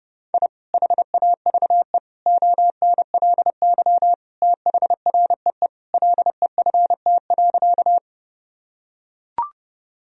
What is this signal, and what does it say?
Morse code, heard as a keyed tone, about 32 wpm, 700 Hz. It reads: I HAVE ONLY THREE LEFT.
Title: Sentences from Top 1000 Words - Review Part 1 - 30wpm